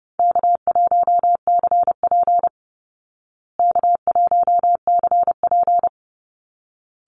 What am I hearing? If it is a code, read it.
Morse code, heard as a keyed tone, about 30 wpm, 700 Hz. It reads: K1CP K1CP
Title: Easy Call Signs Repeated 2x - 30wpm